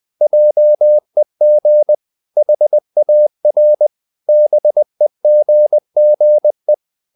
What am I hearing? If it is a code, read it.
Morse code, heard as a keyed tone, about 20 wpm, 600 Hz. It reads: JEG HAR BEGGE
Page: Ferie 20 wpm